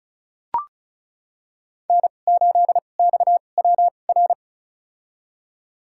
Morse code, heard as a keyed tone, about 36 wpm, spaced as for 25 wpm, 700 Hz. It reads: N8XWR
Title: USA Call Signs - 2x Character Spacing - 35wpm